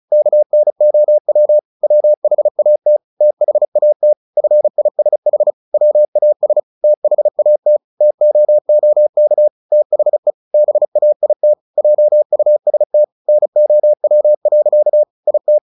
Morse code, heard as a keyed tone, about 34 wpm, 600 Hz. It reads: KNOW WHAT THAT FISH WAS THAT TOOK THE BAIT JUST NOW. IT
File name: the old man and the sea / the old man and the sea 1 35wpm